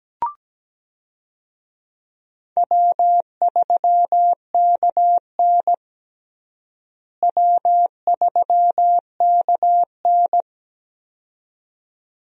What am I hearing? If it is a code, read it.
Morse code, heard as a keyed tone, about 17 wpm, 700 Hz. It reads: W3KN W3KN
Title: USA Call Signs - 4 Characters - Repeated 2x - 17wpm